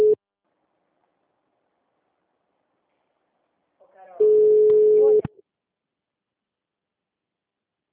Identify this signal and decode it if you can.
Ringback tone (Europe, 425 Hz), with speech.
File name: out- -2014- - - .97463